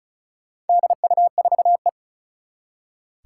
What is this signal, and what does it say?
Morse code, heard as a keyed tone, about 36 wpm, 690 Hz. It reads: DU4E
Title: Four Challenging Letter-Number Combinations 35wpm